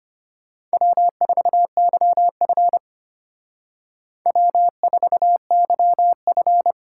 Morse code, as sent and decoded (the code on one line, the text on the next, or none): .-- ....- -.-- ..-. / .-- ....- -.-- ..-.
W4YF W4YF